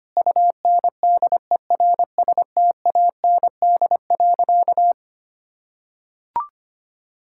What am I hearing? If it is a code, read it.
Morse code, heard as a keyed tone, about 25 wpm, 700 Hz. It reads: UNDERSTAND.
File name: practice / 25wpm / Sentences with 4-Word Phrases - 25wpm